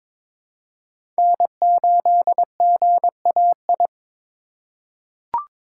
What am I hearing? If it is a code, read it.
Morse code, heard as a keyed tone, about 21 wpm, 700 Hz. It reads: N8GAI
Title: USA Call Signs - 5 Characters - 4x Character Spacing - 22wpm